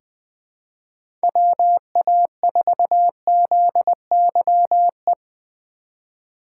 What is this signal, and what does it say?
Morse code, heard as a keyed tone, about 20 wpm, 700 Hz. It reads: WA4ZYE
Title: USA Call Signs - 6 Characters - 2x Character Spacing - 20wpm